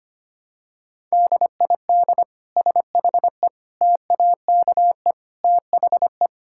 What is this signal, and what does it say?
Morse code, heard as a keyed tone, about 25 wpm, 700 Hz. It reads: DID SHE TAKE THE
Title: Sentences from Top 1000 Words - Review Part 6 - 25wpm